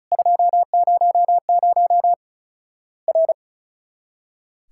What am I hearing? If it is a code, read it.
Morse code, heard as a keyed tone, about 36 wpm, 700 Hz. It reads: 200 R
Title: Worked All Europe DX Contest - QTC Exchange - 35wpm